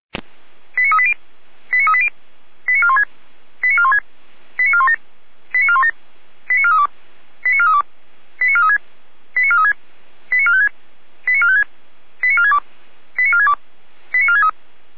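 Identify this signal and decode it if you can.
Five-tone selective call (ZVEI), heard as quick runs of five tones, tones of about 70 ms. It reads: 89380 89380 89416 89416 89417 89417 89432 89432 89436 89436 89456 89456 89462 89462 89463
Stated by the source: ZVEI five-tone selective call received by SDR.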